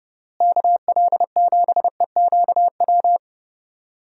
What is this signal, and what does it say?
Morse code, heard as a keyed tone, about 31 wpm, 700 Hz. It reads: KL7EQW
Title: DXCC-100 Call Signs - 6 Characters - Repeated 3x - 30wpm